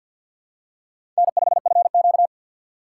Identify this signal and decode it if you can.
Morse code, heard as a keyed tone, about 48 wpm, 700 Hz. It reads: N5FQ
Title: USA Call Signs - 4 Characters - 2x Character Spacing - 50wpm